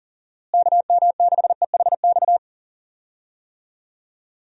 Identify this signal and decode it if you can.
Morse code, heard as a keyed tone, about 40 wpm, 700 Hz. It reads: KM6EHX
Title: USA Call Signs - 6 Characters - Repeated 2x - 40wpm